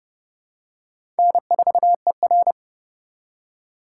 Morse code, heard as a keyed tone, about 30 wpm, 700 Hz. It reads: N4ER
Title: USA Call Signs - 4 Characters - 3x Character Spacing - 30wpm